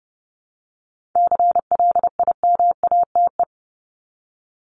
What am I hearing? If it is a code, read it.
Morse code, heard as a keyed tone, about 30 wpm, 700 Hz. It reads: CLIMATE
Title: Top 1501-1750 Words 30wpm